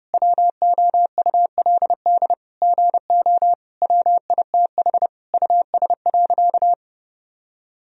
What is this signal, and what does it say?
Morse code, heard as a keyed tone, about 32 wpm, 700 Hz. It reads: WOULD GO WITH US.
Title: Sentences from Top 1000 Words - Review Part 3 - 30wpm